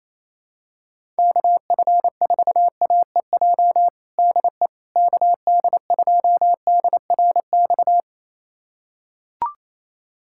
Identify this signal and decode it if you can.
Morse code, heard as a keyed tone, about 28 wpm, 700 Hz. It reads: KF4AEJ DE KD2DRX
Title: USA Callsign DE Callsign - 6 Characters - 28wpm